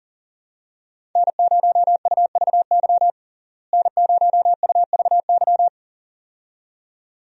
Morse code, heard as a keyed tone, about 40 wpm, 700 Hz. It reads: N0UVY N0UVY
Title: USA Call Signs - 5 Characters - Repeated 2x - 40wpm